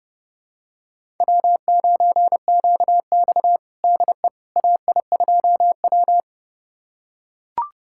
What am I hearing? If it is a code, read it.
Morse code, heard as a keyed tone, about 32 wpm, 700 Hz. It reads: W9QX DE AI2W
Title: USA Callsign DE Callsign - 4 Characters - 30wpm